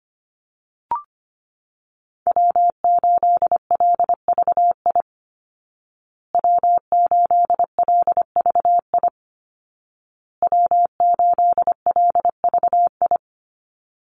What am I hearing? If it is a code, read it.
Morse code, heard as a keyed tone, about 25 wpm, 700 Hz. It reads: W8LVI W8LVI W8LVI
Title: Call Signs Repeated 2x - 25wpm